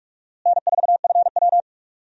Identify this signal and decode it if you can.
Morse code, heard as a keyed tone, about 46 wpm, 700 Hz. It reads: N4FW
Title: USA Call Signs - 4 Characters - Repeated 3x - 45wpm